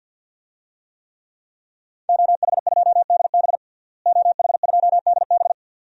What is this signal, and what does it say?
Morse code, heard as a keyed tone, about 52 wpm, 700 Hz. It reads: OH2DB OH2DB
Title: DXCC-100 Call Signs - 5 Characters - Repeated 2x - 50wpm